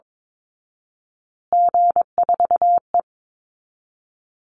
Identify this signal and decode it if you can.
Morse code, heard as a keyed tone, about 22 wpm, 700 Hz. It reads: G4E
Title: Call Signs Repeated 2x - 22wpm